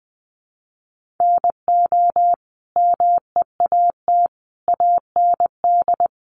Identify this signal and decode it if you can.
Morse code, heard as a keyed tone, about 20 wpm, 700 Hz. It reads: NO MEAT AND
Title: Sentences from Top 1500 Words 20wpm